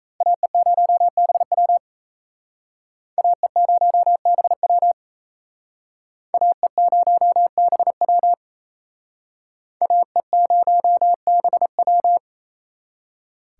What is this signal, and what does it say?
Morse code, heard as a keyed tone, about 37 wpm, 700 Hz. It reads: AE0BW AE0BW AE0BW AE0BW